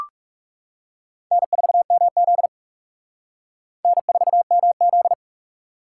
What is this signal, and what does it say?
Morse code, heard as a keyed tone, about 40 wpm, 700 Hz. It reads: N4MZ N4MZ